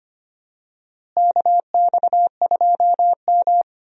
Morse code, heard as a keyed tone, about 25 wpm, 700 Hz. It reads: KX2M
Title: USA Call Signs - 4 Characters - Repeated 3x - 25wpm